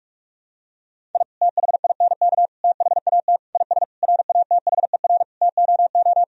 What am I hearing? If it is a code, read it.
Morse code, heard as a keyed tone, about 46 wpm, 700 Hz. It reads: I THINK THAT IS RATHER TOO
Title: Sentences from Top 1000 Words - Review Part 4 - 45wpm